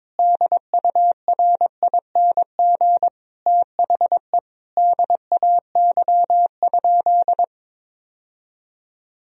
Morse code, heard as a keyed tone, about 21 wpm, 700 Hz. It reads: DURING THE DAY?
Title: Sentences from Top 1000 Words - Review Part 4 - 22wpm